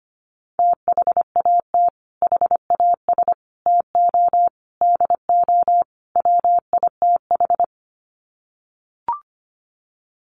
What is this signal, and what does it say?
Morse code, heard as a keyed tone, about 25 wpm, 700 Hz. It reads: THAT HAS TO DO WITH
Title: Sets of 5 Words Encore2 25wpm